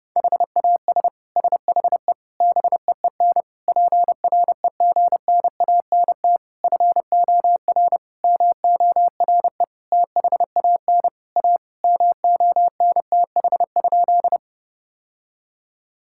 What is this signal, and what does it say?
Morse code, heard as a keyed tone, about 30 wpm, 700 Hz. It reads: HAS SHE BEEN PREGNANT FOR MORE THAN A MONTH?
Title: Sentences from Top 1000 Words - Review Part 6 - 30wpm